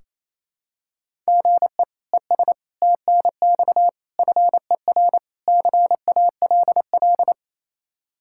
Morse code, heard as a keyed tone, about 28 wpm, 700 Hz. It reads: GE ES TNX FER CALL
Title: K1USN SST QSO Practice 28wpm